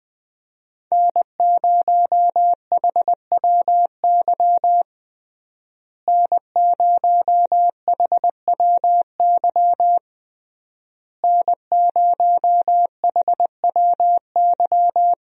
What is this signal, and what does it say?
Morse code, heard as a keyed tone, about 20 wpm, 700 Hz. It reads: N0HWY N0HWY N0HWY
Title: USA Call Signs - 5 Characters - Repeated 3x - 20wpm